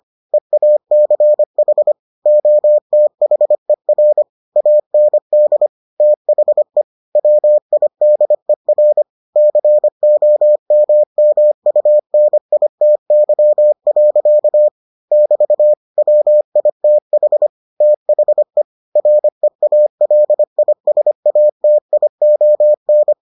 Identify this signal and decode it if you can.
Morse code, heard as a keyed tone, about 25 wpm, 600 Hz. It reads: EACH OTHER AND THE WIDER COMMUNITY. <BT> WITH THE REALISATION